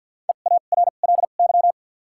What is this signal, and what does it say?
Morse code, heard as a keyed tone, about 52 wpm, 700 Hz. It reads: E A R L Y
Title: 2x Character Spacing - Top 500 Words - 50wpm